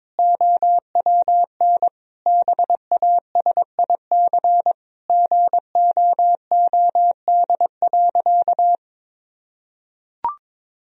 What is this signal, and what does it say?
Morse code, heard as a keyed tone, about 21 wpm, 700 Hz. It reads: OWN BASIC GOOD.
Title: Sentences from Top 1000 Words - Review Part 3 - 22wpm